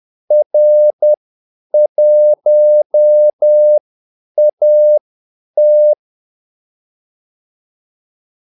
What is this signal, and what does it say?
Morse code, heard as a keyed tone, about 10 wpm, spaced as for 8 wpm, 600 Hz. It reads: R1AT
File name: Callsigns-10-8wpm